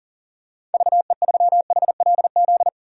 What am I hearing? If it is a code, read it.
Morse code, heard as a keyed tone, about 40 wpm, 700 Hz. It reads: VE3HLZ
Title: DXCC-100 Call Signs - 6 Characters - 40wpm